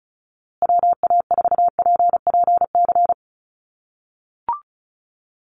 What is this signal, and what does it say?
Morse code, heard as a keyed tone, about 34 wpm, 700 Hz. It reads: WA4PPC
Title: USA Call Signs Repeated 2x 35wpm